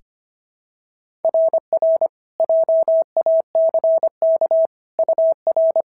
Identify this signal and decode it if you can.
Morse code, heard as a keyed tone, about 22 wpm, 650 Hz. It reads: RR JACK UR
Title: SOTA QSO Practice 25wpm